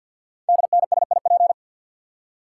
Morse code, heard as a keyed tone, about 51 wpm, 700 Hz. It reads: DNSIP
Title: Code Groups - 5 Letters - 50wpm